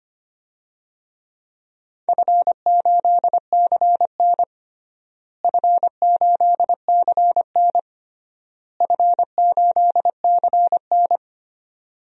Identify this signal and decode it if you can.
Morse code, heard as a keyed tone, about 25 wpm, 700 Hz. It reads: F8CN F8CN F8CN
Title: DXCC-100 Call Signs - 4 Characters - Repeated 3x - 25wpm